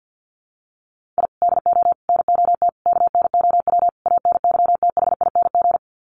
Morse code, heard as a keyed tone, about 50 wpm, 700 Hz. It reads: I DO NOT KNOW ANYTHING
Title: Sets of 5 Words Encore 50wpm